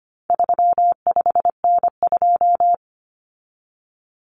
Morse code, heard as a keyed tone, about 25 wpm, 700 Hz. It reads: 35N2
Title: Common Cut Numbers - 4-Digit 25wpm